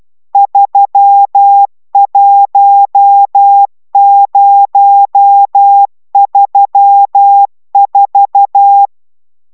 Morse code, heard as a keyed tone, about 12 wpm, 800 Hz. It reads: 31034